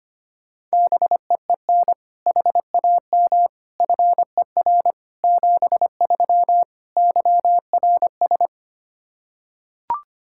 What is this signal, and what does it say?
Morse code, heard as a keyed tone, about 25 wpm, 700 Hz. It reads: BEEN HAM FER 73 YRS
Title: K1USN SST QSO Practice 25wpm